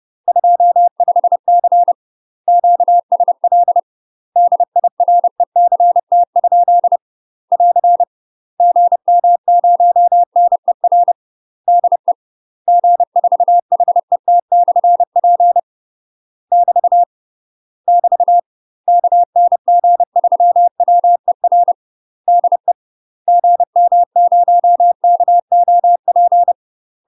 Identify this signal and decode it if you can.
Morse code, heard as a keyed tone, about 30 wpm, 700 Hz. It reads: 25C QSL DIRECT? <AR> GM0NER DE G4HET/P <BT> <BT> KN G3WER DE GM0KOP
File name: UK MASTER QSOs TRAINING / 30WPM / UK MASTER QSOs 0043